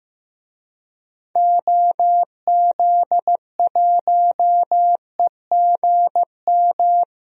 Morse code, heard as a keyed tone, about 15 wpm, 700 Hz. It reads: OZ1EGM
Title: DXCC-100 Call Signs - 6 Characters - Repeated 3x - 15wpm